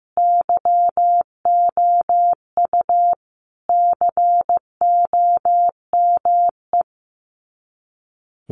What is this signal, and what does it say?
Morse code, heard as a keyed tone, about 15 wpm, 700 Hz. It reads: YOU COME
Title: Sets of 5 Words Encore2 15wpm